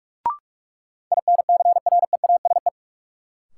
Morse code, heard as a keyed tone, about 46 wpm, 700 Hz. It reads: INCREASE
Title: Sets of 5 Words - New Word Drill - 45wpm